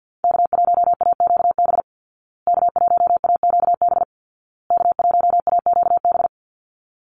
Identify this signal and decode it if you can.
Morse code, heard as a keyed tone, about 50 wpm, 700 Hz. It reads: K1AQB K1AQB K1AQB
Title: Call Signs Repeated 2x - 50wpm